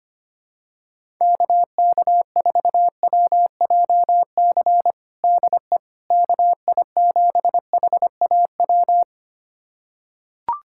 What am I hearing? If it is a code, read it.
Morse code, heard as a keyed tone, about 25 wpm, 700 Hz. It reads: KK4WJC DE KI7HAW
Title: USA Callsign DE Callsign - 6 Characters - 25wpm